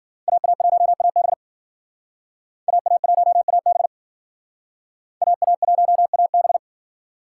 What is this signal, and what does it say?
Morse code, heard as a keyed tone, about 54 wpm, 700 Hz. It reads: AA1AB AA1AB AA1AB